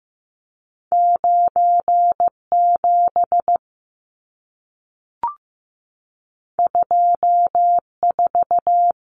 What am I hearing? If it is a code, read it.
Morse code, heard as a keyed tone, about 15 wpm, 700 Hz. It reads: 97 24
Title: Numbers - 2-Digit 15wpm